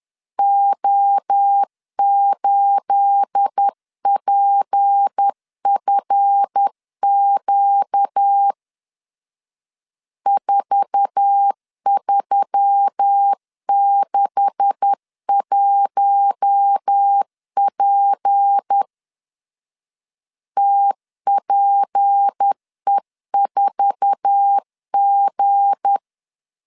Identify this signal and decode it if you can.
Morse code, heard as a keyed tone, about 10 wpm, 800 Hz. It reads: O8PFQ 4361P TPE4G
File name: CW / DET / 1 9 1500C S B 10wpm 4439